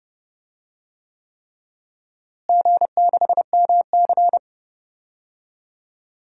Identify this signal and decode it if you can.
Morse code, heard as a keyed tone, about 30 wpm, 700 Hz. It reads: G6MC
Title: DXCC-100 Call Signs - 4 Characters - Repeated 3x - 30wpm